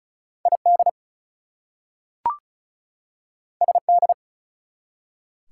Morse code, heard as a keyed tone, about 36 wpm, 700 Hz. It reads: ID SD
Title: US States and Canadian Provinces 35wpm